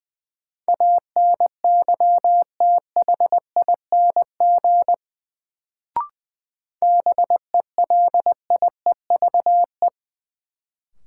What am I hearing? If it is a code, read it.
Morse code, heard as a keyed tone, about 20 wpm, 700 Hz. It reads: ANYTHING BELIEVE
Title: Sentences from Top 1000 Words - New Word Drill - 20wpm